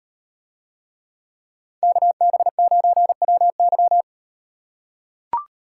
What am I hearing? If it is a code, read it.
Morse code, heard as a keyed tone, about 38 wpm, 700 Hz. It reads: KB9WY